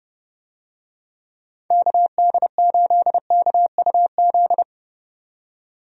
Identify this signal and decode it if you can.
Morse code, heard as a keyed tone, about 29 wpm, 700 Hz. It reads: KD8KUZ
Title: USA Call Signs - 6 Characters - 30wpm